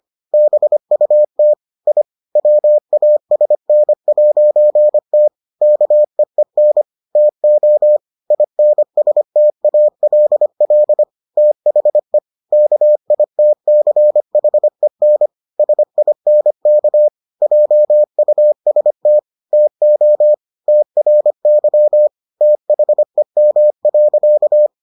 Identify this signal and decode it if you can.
Morse code, heard as a keyed tone, about 25 wpm, 600 Hz. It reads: BUT I WASN'T KEEN TO INSTALL THE KITCHEN SINK JUST TO TRY THEM.